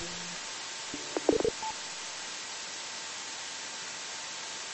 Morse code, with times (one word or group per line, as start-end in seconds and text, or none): none